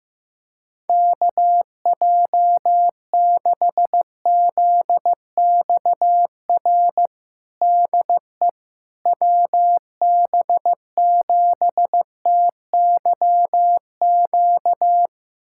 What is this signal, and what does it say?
Morse code, heard as a keyed tone, about 15 wpm, 700 Hz. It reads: KJ6ZXR DE WB7TYQ
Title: USA Callsign DE Callsign - 6 Characters - 15wpm